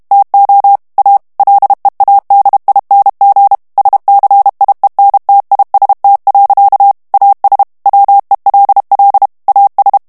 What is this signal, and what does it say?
Morse code, heard as a keyed tone, about 32 wpm, 800 Hz. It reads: TO A LEADING SCIENTIST. AS WELL AS
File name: ANGLAIS texte clair 2 vitesse 30 mots minute